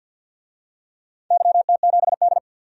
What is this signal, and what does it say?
Morse code, heard as a keyed tone, about 47 wpm, 700 Hz. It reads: YT7D
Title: DXCC-100 Call Signs - 4 Characters - 50wpm